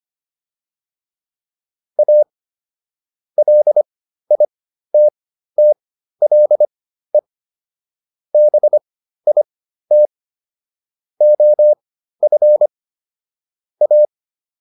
Morse code, heard as a keyed tone, about 25 wpm, spaced as for 13 wpm, 600 Hz. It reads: A LITTLE BIT OF A